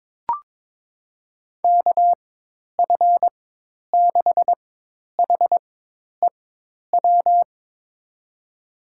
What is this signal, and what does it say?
Morse code, heard as a keyed tone, about 22 wpm, spaced as for 10 wpm, 700 Hz. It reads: KF6HEW
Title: USA Call Signs - 4x Character Spacing - 22wpm